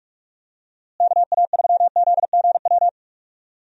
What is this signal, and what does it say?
Morse code, heard as a keyed tone, about 41 wpm, 700 Hz. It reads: KA3ZGW
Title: USA Call Signs - 6 Characters - Repeated 3x - 45wpm